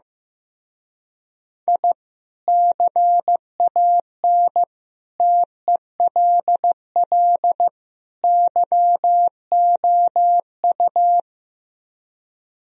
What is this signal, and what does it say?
Morse code, heard as a keyed tone, about 15 wpm, 700 Hz. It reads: I CAN TELL YOU
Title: Speed-Racing - Sets of 4 Words 15wpm